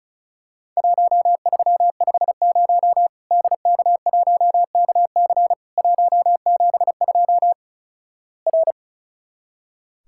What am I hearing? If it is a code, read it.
Morse code, heard as a keyed tone, about 36 wpm, 700 Hz. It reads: 1350 DK1KC 172 R
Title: Worked All Europe DX Contest - QTC Exchange - 35wpm